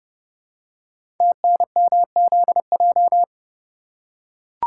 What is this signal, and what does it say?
Morse code, heard as a keyed tone, about 30 wpm, 700 Hz. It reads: TNMZJ
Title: Code Groups - 5 Letters - 30wpm